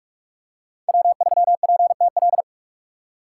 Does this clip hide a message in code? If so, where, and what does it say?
Yes: Morse code from 0.9 to 2.4 s, heard as a keyed tone, about 44 wpm, 700 Hz. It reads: W3PTL